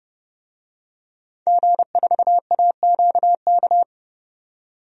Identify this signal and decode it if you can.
Morse code, heard as a keyed tone, about 29 wpm, 700 Hz. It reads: G4AQK
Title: DXCC-100 Call Signs - 5 Characters - 30wpm